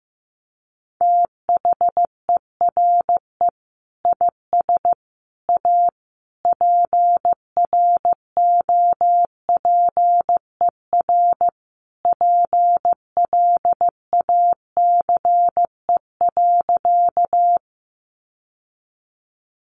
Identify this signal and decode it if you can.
Morse code, heard as a keyed tone, about 15 wpm, 700 Hz. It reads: THERE IS A PROPER PLACE.
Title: Sentences from Top 800 Words 15wpm